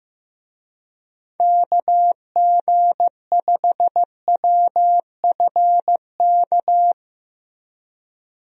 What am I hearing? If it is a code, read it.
Morse code, heard as a keyed tone, about 16 wpm, 700 Hz. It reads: KG5WFK
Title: USA Call Signs - 6 Characters - Repeated 2x - 15wpm